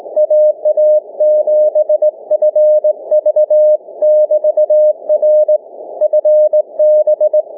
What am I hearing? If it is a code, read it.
Morse code, heard as a keyed tone, about 20 wpm, 610 Hz. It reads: AA7FV <BT> R FB
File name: cw / ssn 30M